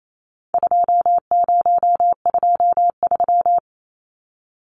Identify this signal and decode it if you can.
Morse code, heard as a keyed tone, about 28 wpm, 700 Hz. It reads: 2023
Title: Numbers - 4-Digit 28wpm